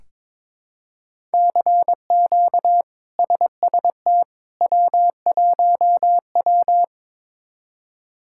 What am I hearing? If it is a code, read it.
Morse code, heard as a keyed tone, about 22 wpm, 700 Hz. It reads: CQ SST W1W
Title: K1USN SST QSO Practice - 22wpm